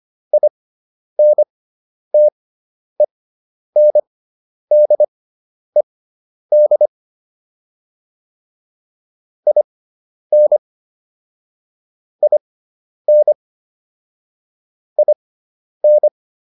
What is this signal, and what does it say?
Morse code, heard as a keyed tone, about 25 wpm, spaced as for 10 wpm, 600 Hz. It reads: INTENDED IN IN IN